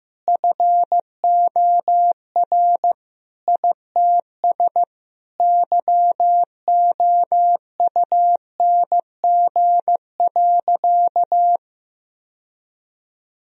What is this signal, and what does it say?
Morse code, heard as a keyed tone, about 15 wpm, 700 Hz. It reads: FOR ITS YOUNG.
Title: Sentences from Top 1000 Words - Review Part 2 - 15wpm